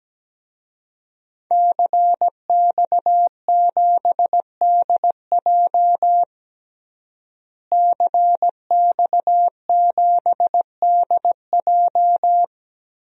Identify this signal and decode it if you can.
Morse code, heard as a keyed tone, about 17 wpm, 700 Hz. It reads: CX7DJ CX7DJ
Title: DXCC-100 Call Signs - 5 Characters - Repeated 3x - 17wpm